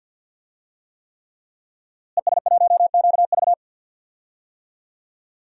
Morse code, heard as a keyed tone, about 52 wpm, 700 Hz. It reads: ES1QV
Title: DXCC-100 Call Signs - 5 Characters - Repeated 3x - 50wpm